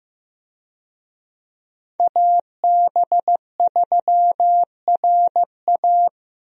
Morse code, heard as a keyed tone, about 15 wpm, 700 Hz. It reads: AB3RA
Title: USA Call Signs - 5 Characters - Repeated 3x - 15wpm